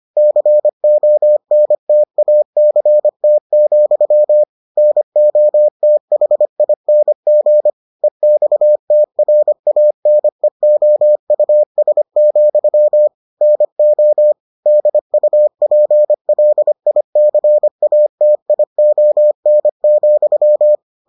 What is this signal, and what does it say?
Morse code, heard as a keyed tone, about 25 wpm, 600 Hz. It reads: CONTACT, NOTHING EXTRANEOUS, NO DUPLICATION,